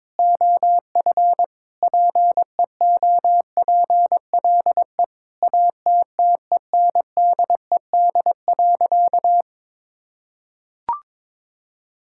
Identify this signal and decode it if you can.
Morse code, heard as a keyed tone, about 22 wpm, 700 Hz. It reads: OF PEOPLE ATTENDED.
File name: practice / 22wpm / Sentences with 4-Word Phrases - 22wpm